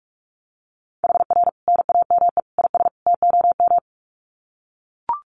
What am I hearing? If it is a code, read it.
Morse code, heard as a keyed tone, about 44 wpm, 700 Hz. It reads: HR NAME IS TOM
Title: Common QSO Phrases 45wpm